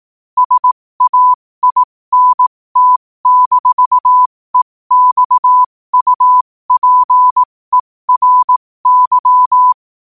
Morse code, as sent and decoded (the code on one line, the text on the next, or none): ... .- .. -. - -....- . -..- ..- .--. .-. -.--
SAINT-EXUPERY